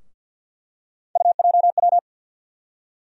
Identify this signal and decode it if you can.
Morse code, heard as a keyed tone, about 52 wpm, 700 Hz. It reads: UJW
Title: USA Call Sign Postfixes - 3 Letters 50wpm